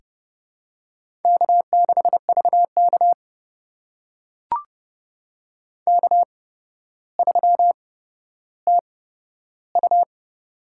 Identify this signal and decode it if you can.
Morse code, heard as a keyed tone, about 30 wpm, spaced as for 13 wpm, 700 Hz. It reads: K6VK K3TU
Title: USA Call Signs - 4 Characters - 8x Character Spacing - 30wpm